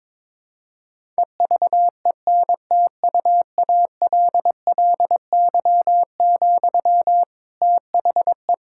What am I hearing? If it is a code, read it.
Morse code, heard as a keyed tone, about 22 wpm, 700 Hz. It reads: EVENTUALLY, THE
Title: Sentences from Top 1000 Words - Review Part 6 - 22wpm